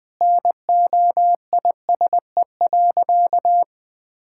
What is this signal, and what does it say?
Morse code, heard as a keyed tone, about 21 wpm, 700 Hz. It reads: NOISE.
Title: Sentences with 2-Word Phrases - Spoken Beginning - 20wpm